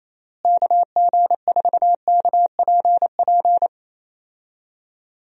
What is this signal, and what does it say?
Morse code, heard as a keyed tone, about 27 wpm, 700 Hz. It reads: KG4KPP
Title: USA Call Signs - 6 Characters - Repeated 3x - 28wpm